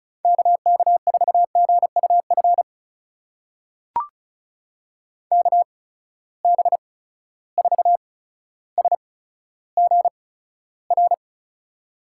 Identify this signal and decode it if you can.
Morse code, heard as a keyed tone, about 36 wpm, spaced as for 15 wpm, 700 Hz. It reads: KK4GUF KB4SGR
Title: USA Call Signs - 6 Characters - 8x Character Spacing - 35wpm